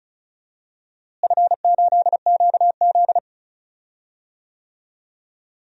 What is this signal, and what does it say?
Morse code, heard as a keyed tone, about 34 wpm, 700 Hz. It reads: F8QZ
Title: DXCC-100 Call Signs - 4 Characters - Repeated 2x - 35wpm